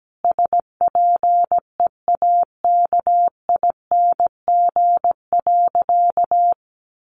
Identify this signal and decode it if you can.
Morse code, heard as a keyed tone, about 17 wpm, 700 Hz. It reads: SPEAKING.
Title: A Sound Of Thunder 17wpm